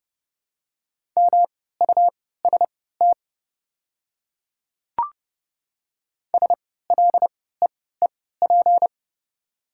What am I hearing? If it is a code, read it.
Morse code, heard as a keyed tone, about 32 wpm, spaced as for 17 wpm, 700 Hz. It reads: MUST SLEEP
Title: 3x Character Spacing - Top 500 Words - 30wpm